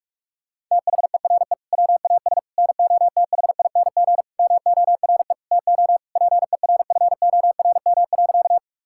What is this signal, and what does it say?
Morse code, heard as a keyed tone, about 46 wpm, 700 Hz. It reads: THERE WAS NOTHING MORE TO PERFORM.
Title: Sentences from Top 1000 Words - Review Part 5 - 45wpm